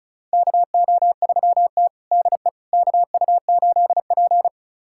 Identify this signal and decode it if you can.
Morse code, heard as a keyed tone, about 36 wpm, 700 Hz. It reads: KO3T DE KU8P
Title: USA Callsign DE Callsign - 4 Characters - 35wpm